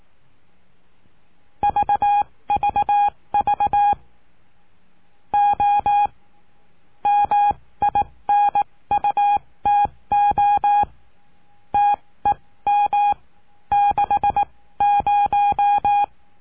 Morse code, heard as a keyed tone, about 18 wpm, spaced as for 15 wpm, 820 Hz. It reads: VVV O MINUTO TEM 60
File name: cursos / CURSO DE TELEGRAFIA / Telegrafia - Trilha 20